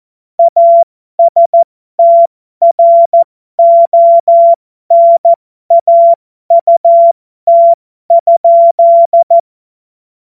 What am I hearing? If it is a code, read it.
Morse code, heard as a keyed tone, about 14 wpm, spaced as for 12 wpm, 680 Hz. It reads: ASTRONAUT?
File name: Morsekurs / Witze